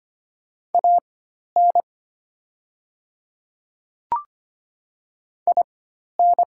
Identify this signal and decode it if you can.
Morse code, heard as a keyed tone, about 25 wpm, spaced as for 12 wpm, 700 Hz. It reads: AN IN
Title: 4x Character Spacing - Top 200 Words - 25wpm